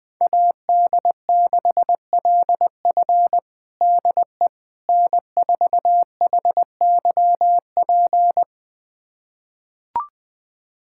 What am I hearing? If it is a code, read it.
Morse code, heard as a keyed tone, about 20 wpm, 700 Hz. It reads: AD6LF DE N4HYP
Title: USA Callsign DE Callsign - 5 Characters - 20wpm